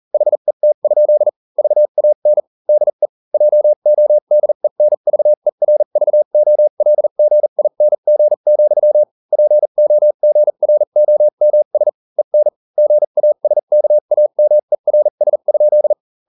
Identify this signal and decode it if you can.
Morse code, heard as a keyed tone, about 40 wpm, spaced as for 35 wpm, 600 Hz. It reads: HET? VAN DE JODENVERVOLGING, POGROMS EN GASKAMERS?